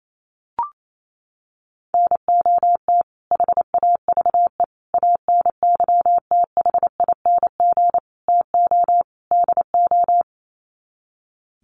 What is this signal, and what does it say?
Morse code, heard as a keyed tone, about 28 wpm, 700 Hz. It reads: NOT HAVE ANYTHING TO DO
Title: Sets of 5 Words Encore 28wpm